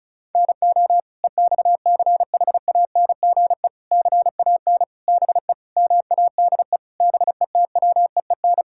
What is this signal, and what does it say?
Morse code, heard as a keyed tone, about 36 wpm, 700 Hz. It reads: NO EXCHANGE CAN BE MADE BETWEEN
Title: Sentences from Top 1000 Words - Review Part 6 - 35wpm